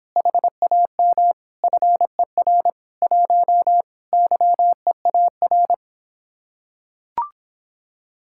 Morse code, heard as a keyed tone, about 26 wpm, 700 Hz. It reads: HAM FER 1 YEAR